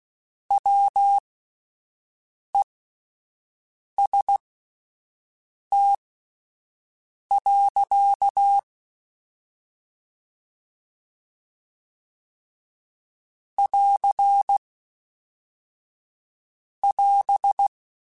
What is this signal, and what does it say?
Morse code, heard as a keyed tone, about 16 wpm, spaced as for 5 wpm, 800 Hz. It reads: WEST. <AR><AS>
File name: Spaced Characters / Spaced Characters-mp3s / Weather 5 1 sp @10